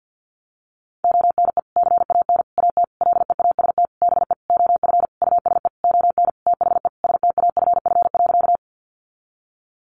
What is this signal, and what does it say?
Morse code, heard as a keyed tone, about 50 wpm, 700 Hz. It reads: ONE CAN AT LEAST BE OF USE ON THE STAFF.
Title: Sentences from Top 1200 Words 50wpm